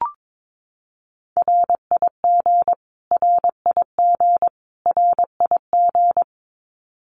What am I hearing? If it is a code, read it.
Morse code, heard as a keyed tone, about 22 wpm, 700 Hz. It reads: RIG RIG RIG
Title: Abbreviations Repeated 3x - 22wpm